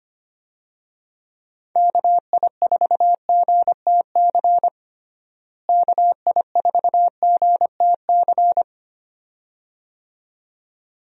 Morse code, heard as a keyed tone, about 25 wpm, 700 Hz. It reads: KI4GTC KI4GTC
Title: USA Call Signs - 6 Characters - Repeated 2x - 25wpm